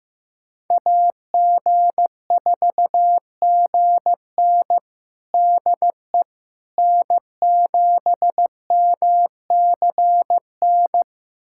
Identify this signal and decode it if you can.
Morse code, heard as a keyed tone, about 15 wpm, 700 Hz. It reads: AG4GN DE N7MCN
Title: USA Callsign DE Callsign - 5 Characters - 15wpm